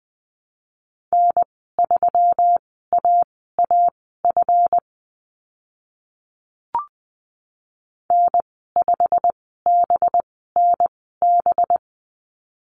Morse code, heard as a keyed tone, about 20 wpm, spaced as for 14 wpm, 700 Hz. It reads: N3AAF N5BNB
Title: 193 Lesson 31g - Letter v - Call Signs 20wpm